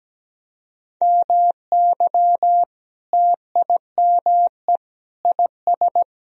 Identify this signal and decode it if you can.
Morse code, heard as a keyed tone, about 17 wpm, 700 Hz. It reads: MY TIME IS
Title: Sentences from Top 1000 Words - Review Part 2 - 17wpm